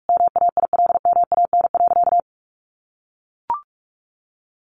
Morse code, heard as a keyed tone, about 44 wpm, 700 Hz. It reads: MAILMAN.
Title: Additional Easy Phrases 45wpm